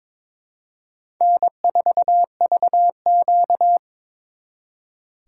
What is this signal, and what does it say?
Morse code, heard as a keyed tone, about 26 wpm, 700 Hz. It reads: N4VQ
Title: Speed Racing - USA 1x2 Call Signs 22wpm